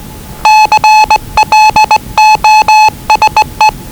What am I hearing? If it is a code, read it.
Morse code, heard as a keyed tone, about 20 wpm, 860 Hz. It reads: CLOSE
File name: WellingtonRadioZlwCloseDownOnHfAndMf / 01-Zlw 4277khz 417.5 qrt until 0925z